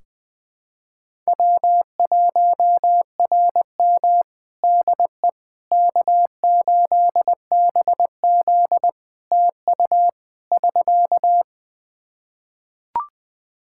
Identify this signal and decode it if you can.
Morse code, heard as a keyed tone, about 20 wpm, 700 Hz. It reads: W1RM DE K8BZ TU <SK>
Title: Practice QSO - Exchange 4 - Ending - 20wpm